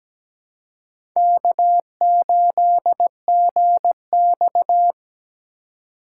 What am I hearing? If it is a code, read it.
Morse code, heard as a keyed tone, about 17 wpm, 700 Hz. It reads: K8GX
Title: USA Call Signs - 4 Characters - 3x Character Spacing - 17wpm